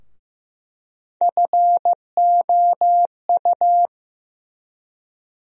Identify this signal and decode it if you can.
Morse code, heard as a keyed tone, about 15 wpm, 700 Hz. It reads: FOU
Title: USA Call Sign Postfixes - 3 Letters 15wpm